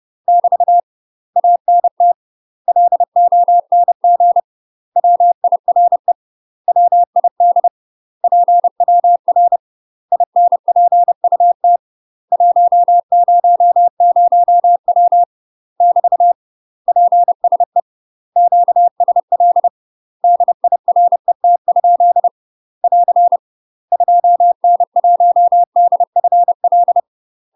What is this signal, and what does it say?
Morse code, heard as a keyed tone, about 30 wpm, 700 Hz. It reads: <BT> ANT LONG WIRE WID PWR INPUT 100W <BT> PSE QSL DIRET? <AR> 2N1DFL
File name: UK MASTER QSOs TRAINING/30WPM